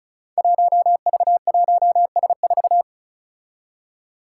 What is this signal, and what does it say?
Morse code, heard as a keyed tone, about 36 wpm, 700 Hz. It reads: 1V1S4
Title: Five Challenging Letter-Number Combinations 35wpm